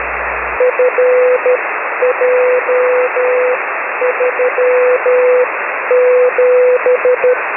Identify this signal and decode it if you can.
Morse code, heard as a keyed tone, about 12 wpm, spaced as for 10 wpm, 500 Hz, with radio noise. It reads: FJ37
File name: 6m-hk6frc-b